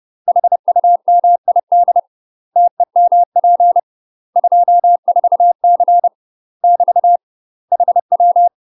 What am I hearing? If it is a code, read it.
Morse code, heard as a keyed tone, about 30 wpm, 700 Hz. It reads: HUMID TEMP 24C <BT> HW
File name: UK MASTER QSOs TRAINING / 30WPM / UK MASTER QSOs 0042